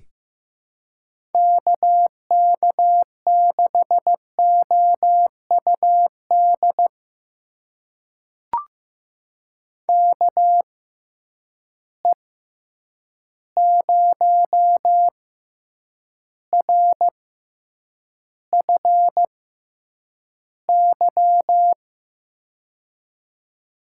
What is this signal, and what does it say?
Morse code, heard as a keyed tone, about 16 wpm, spaced as for 8 wpm, 700 Hz. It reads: KK6OUD KE0RFY
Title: USA Call Signs - 6 Characters - 6x Character Spacing - 15wpm